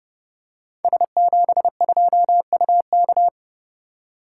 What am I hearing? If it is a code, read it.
Morse code, heard as a keyed tone, about 30 wpm, 700 Hz. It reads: S72UK
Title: Code Groups - 5 Letter:Numbers - 30wpm